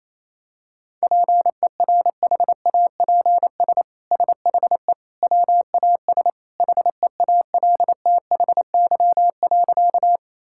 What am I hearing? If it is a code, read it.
Morse code, heard as a keyed tone, about 28 wpm, 700 Hz. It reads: PERHAPS SHE WAS HEALTHY.
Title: Sentences from Top 1000 Words - Review Part 5 - 28wpm